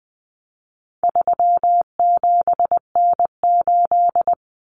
Morse code, heard as a keyed tone, about 20 wpm, 700 Hz. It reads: 37N8
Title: Common Cut Numbers - 4-Digit 20wpm